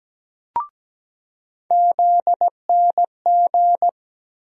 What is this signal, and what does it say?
Morse code, heard as a keyed tone, about 17 wpm, 700 Hz. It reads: ZNG
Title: USA Call Sign Postfixes - 3 Letters 17wpm